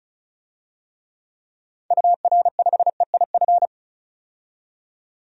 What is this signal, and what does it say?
Morse code, heard as a keyed tone, about 34 wpm, 700 Hz. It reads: UR5EIF
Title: DXCC-100 Call Signs - 6 Characters - 35wpm